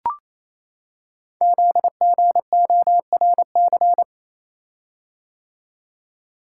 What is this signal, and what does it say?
Morse code, heard as a keyed tone, about 27 wpm, 700 Hz. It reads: ZGORC
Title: Code Groups - 5 Letters - 28wpm